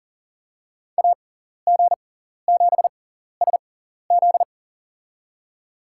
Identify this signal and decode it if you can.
Morse code, heard as a keyed tone, about 40 wpm, spaced as for 14 wpm, 700 Hz. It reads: AG7SZ
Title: USA Call Signs - 5 Characters - 6x Character Spacing - 40wpm